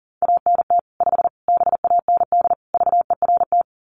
Morse code, heard as a keyed tone, about 40 wpm, 700 Hz. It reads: ANT 5 BAND VERT
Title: Common QSO Phrases 40wpm